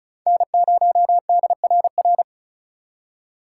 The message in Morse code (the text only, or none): N0DRR